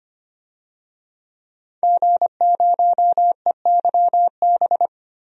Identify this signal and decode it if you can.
Morse code, heard as a keyed tone, about 25 wpm, 700 Hz. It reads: G0EYB
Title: DXCC-100 Call Signs - 5 Characters - 25wpm